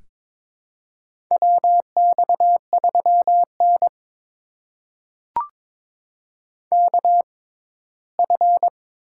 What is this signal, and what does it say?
Morse code, heard as a keyed tone, about 22 wpm, spaced as for 15 wpm, 700 Hz. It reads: WX3N KF
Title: USA Call Signs - 6x Character Spacing - 22wpm